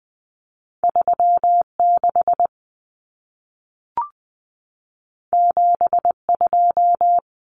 Morse code, heard as a keyed tone, about 20 wpm, 700 Hz. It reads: 36 72
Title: Common Cut Numbers - 2-Digit 20wpm